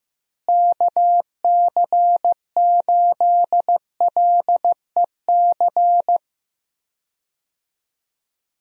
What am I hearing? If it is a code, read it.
Morse code, heard as a keyed tone, about 15 wpm, 700 Hz. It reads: KC8LEC
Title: USA Call Signs - 6 Characters - Repeated 3x - 15wpm